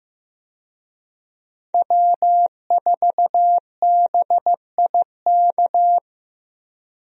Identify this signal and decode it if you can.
Morse code, heard as a keyed tone, about 15 wpm, 700 Hz. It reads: W4BIK
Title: USA Call Signs - 5 Characters - Repeated 2x - 15wpm